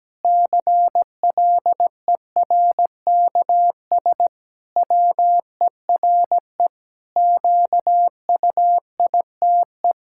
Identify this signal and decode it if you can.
Morse code, heard as a keyed tone, about 17 wpm, 700 Hz. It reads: CLERKS WERE QUITE
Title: Pangram Challenge v2 - 17wpm